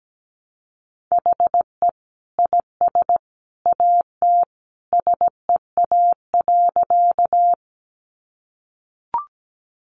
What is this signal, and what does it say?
Morse code, heard as a keyed tone, about 17 wpm, 700 Hz. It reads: HE IS AT SEA.
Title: Sentences from Top 300 Words 17wpm